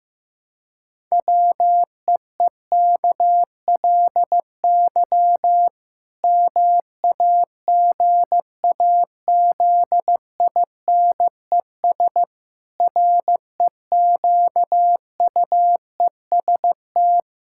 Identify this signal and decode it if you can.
Morse code, heard as a keyed tone, about 15 wpm, 700 Hz. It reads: WEEKLY MAGAZINES REQUEST
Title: Pangram Challenge v2 - 15wpm